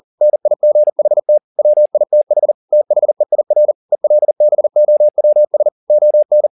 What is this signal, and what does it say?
Morse code, heard as a keyed tone, about 40 wpm, 600 Hz. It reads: NIGHT WITH THEIR ELBOWS ON